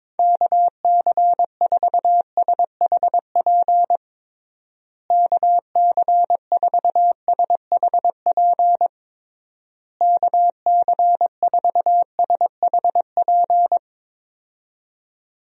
Morse code, heard as a keyed tone, about 22 wpm, 700 Hz. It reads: KC4SHP KC4SHP KC4SHP
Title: USA Call Signs - 6 Characters - Repeated 3x - 22wpm